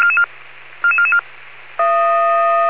Five-tone selective call (ZVEI), heard as quick runs of five tones, tones of about 70 ms. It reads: 44444